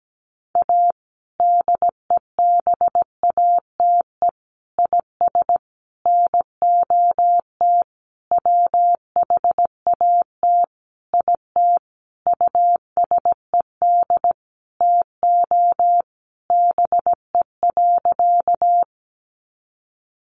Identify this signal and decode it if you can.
Morse code, heard as a keyed tone, about 17 wpm, 700 Hz. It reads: A DEBATE IS NOT WHAT IT USED TO BE.
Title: Sentences from Top 1100 Words 17wpm